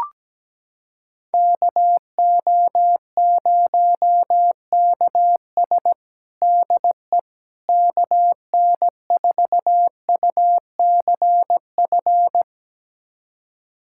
Morse code, heard as a keyed tone, about 17 wpm, 700 Hz. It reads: KO0KS DE KN4UCF
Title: USA Callsign DE Callsign 17wpm